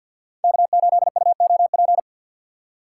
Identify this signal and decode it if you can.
Morse code, heard as a keyed tone, about 66 wpm, spaced as for 57 wpm, 700 Hz. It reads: K8UOP
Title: Speed Racing - USA 1x3 Call Signs 50wpm